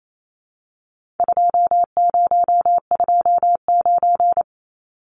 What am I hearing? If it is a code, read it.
Morse code, heard as a keyed tone, about 28 wpm, 700 Hz. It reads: 2029
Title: Numbers - 4-Digit 28wpm